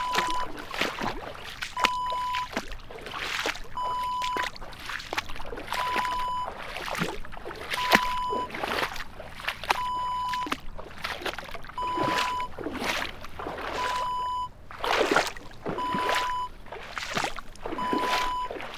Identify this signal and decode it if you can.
There is no code.